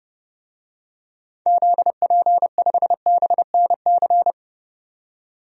Morse code, heard as a keyed tone, about 30 wpm, 700 Hz. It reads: ZP5BNC
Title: DXCC-100 Call Signs - 6 Characters - 30wpm